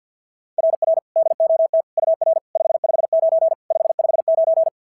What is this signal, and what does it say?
Morse code, heard as a keyed tone, about 46 wpm, 660 Hz. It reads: RR DOT UR 559 559
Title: SOTA QSO Practice 50wpm